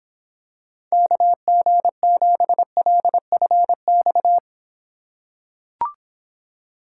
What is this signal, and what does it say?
Morse code, heard as a keyed tone, about 22 wpm, 700 Hz. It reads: KG7LFX